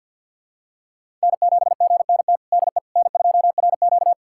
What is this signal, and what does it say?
Morse code, heard as a keyed tone, about 52 wpm, 700 Hz. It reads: N7GNT DE N2RQ
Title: USA Callsign DE Callsign 50wpm